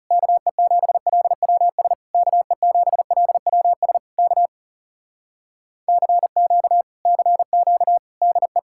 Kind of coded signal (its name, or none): Morse code